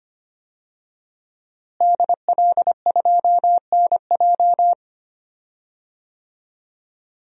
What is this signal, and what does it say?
Morse code, heard as a keyed tone, about 25 wpm, 700 Hz. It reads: DL2NJ
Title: DXCC-100 Call Signs - 5 Characters - Repeated 2x - 25wpm